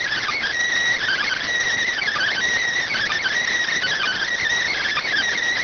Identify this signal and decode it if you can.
Emergency Alert System (SAME) tones, heard as harsh buzzing bursts. It is heard from 1.2 to 5.2 s.